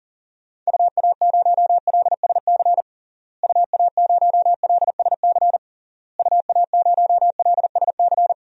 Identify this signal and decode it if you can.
Morse code, heard as a keyed tone, about 40 wpm, 700 Hz. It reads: UA0LSC UA0LSC UA0LSC
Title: DXCC-100 Call Signs - 6 Characters - Repeated 3x - 40wpm